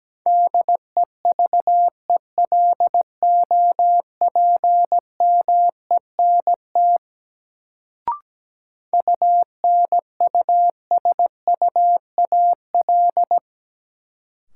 Morse code, heard as a keyed tone, about 17 wpm, 700 Hz. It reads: DEVELOPMENT UNUSUAL
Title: Sentences from Top 1400 Words - New Word Drill Alt - 17wpm